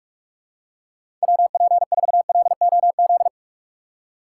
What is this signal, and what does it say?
Morse code, heard as a keyed tone, about 44 wpm, 700 Hz. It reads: WP4LOZ
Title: USA Call Signs - 6 Characters - 2x Character Spacing - 45wpm